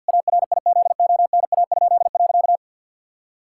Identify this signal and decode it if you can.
Morse code, heard as a keyed tone, about 47 wpm, 700 Hz. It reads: ARIZONA?.
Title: Pangram Challenge v2 - 50wpm